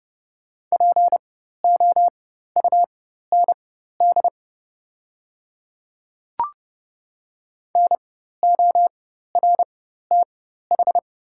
Morse code, heard as a keyed tone, about 29 wpm, spaced as for 14 wpm, 690 Hz. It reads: POUND NORTH
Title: 4x Character Spacing - Top 500 Words - 30wpm